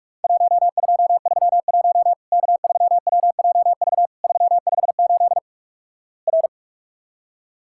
Morse code, heard as a keyed tone, about 45 wpm, 700 Hz. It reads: 1231 K3WJV 358 R
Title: Worked All Europe DX Contest - QTC Exchange - 45wpm